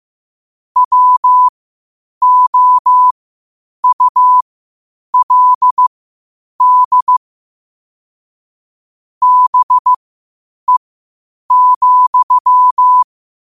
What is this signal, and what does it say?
Morse code, heard as a keyed tone, about 15 wpm, spaced as for 8 wpm, 1000 Hz. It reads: WOULD BE,